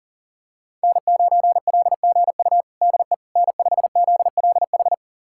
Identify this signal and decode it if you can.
Morse code, heard as a keyed tone, about 40 wpm, 700 Hz. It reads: N9LGU DE N5ZLH
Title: USA Callsign DE Callsign - 5 Characters - 40wpm